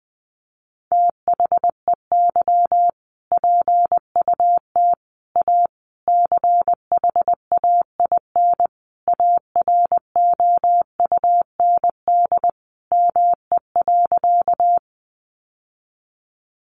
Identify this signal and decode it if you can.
Morse code, heard as a keyed tone, about 20 wpm, 700 Hz. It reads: THEY PUT A CHAIN AROUND ME.
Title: Sentences from Top 1500 Words 20wpm